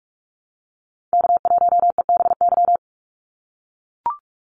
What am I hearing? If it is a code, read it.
Morse code, heard as a keyed tone, about 44 wpm, 700 Hz. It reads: K1EBY
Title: Call Signs - 45wpm